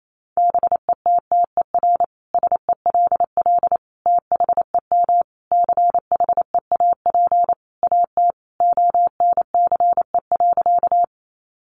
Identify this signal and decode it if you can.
Morse code, heard as a keyed tone, about 28 wpm, 700 Hz. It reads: BETTER SELL THEM CHEAP AT ONCE.
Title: Sentences from Top 1200 Words 28wpm